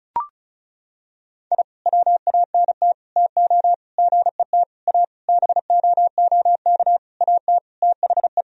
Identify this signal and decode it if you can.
Morse code, heard as a keyed tone, about 36 wpm, 700 Hz. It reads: I WANT TO GET A BOOK AT THE
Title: Sentences from Top 1000 Words - Review Part 1 - 35wpm